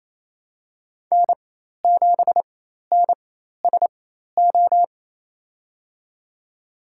Morse code, heard as a keyed tone, about 28 wpm, spaced as for 13 wpm, 700 Hz. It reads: N7NSO
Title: USA Call Signs - 5 Characters - 4x Character Spacing - 28wpm